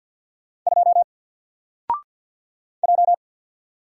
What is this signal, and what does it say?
Morse code, heard as a keyed tone, about 48 wpm, 700 Hz. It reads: 2 J
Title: Single Letter-Number Alt 50wpm